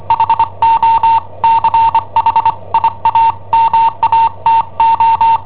Morse code, heard as a keyed tone, about 25 wpm, 910 Hz. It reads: HO CHIAMATO
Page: Cassetta esercizi preparata da I7OHP per IW5ELP